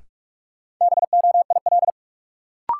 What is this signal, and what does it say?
Morse code, heard as a keyed tone, about 46 wpm, 700 Hz. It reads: BOIL
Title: Sets of 5 Words Encore - New Word Drill - 45wpm